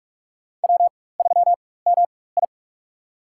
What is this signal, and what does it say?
Morse code, heard as a keyed tone, about 46 wpm, spaced as for 21 wpm, 700 Hz. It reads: W3MI
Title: USA Call Signs - 4 Characters - 4x Character Spacing - 45wpm